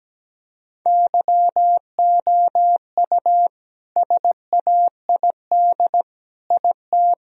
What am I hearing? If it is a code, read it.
Morse code, heard as a keyed tone, about 17 wpm, 700 Hz. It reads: YOU SAID IT
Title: Sentences from Top 1000 Words - Review Part 2 - 17wpm